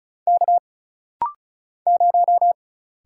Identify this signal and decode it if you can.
Morse code, heard as a keyed tone, about 35 wpm, 700 Hz. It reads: K 0
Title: Single Letter-Number - Rapid-Fire 35wpm